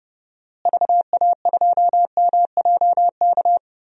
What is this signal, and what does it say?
Morse code, heard as a keyed tone, about 30 wpm, 700 Hz. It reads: VA2MJK
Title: DXCC-100 Call Signs - 6 Characters - Repeated 3x - 30wpm